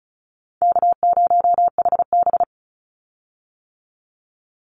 Morse code, heard as a keyed tone, about 34 wpm, 700 Hz. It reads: K0HB
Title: Call Signs - 35wpm